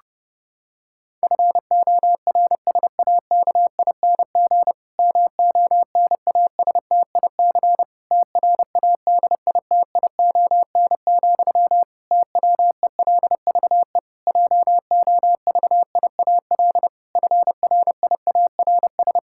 Morse code, heard as a keyed tone, about 30 wpm, 700 Hz. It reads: FORSAKING MONASTIC TRADITION, TWELVE JOVIAL FRIARS
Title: Pangram Challenge v2 - 30wpm